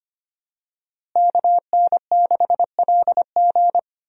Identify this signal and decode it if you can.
Morse code, heard as a keyed tone, about 25 wpm, 700 Hz. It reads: KN6LG
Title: USA Call Signs - 5 Characters - Repeated 2x - 25wpm